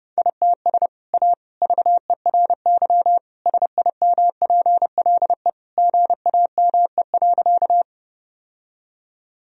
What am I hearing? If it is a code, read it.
Morse code, heard as a keyed tone, about 32 wpm, 700 Hz. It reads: ITS A VERY SIMPLE GAME.
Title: Sentences from Top 1000 Words - Review Part 3 - 30wpm